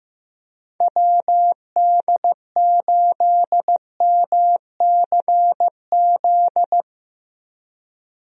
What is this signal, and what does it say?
Morse code, heard as a keyed tone, about 15 wpm, 700 Hz. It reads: WD8MCZ
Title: USA Call Signs - 6 Characters - 15wpm